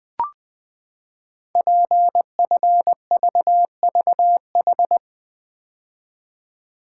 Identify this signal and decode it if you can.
Morse code, heard as a keyed tone, about 20 wpm, 700 Hz. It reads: PFVVH
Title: Code Groups - 5 Letters - 20wpm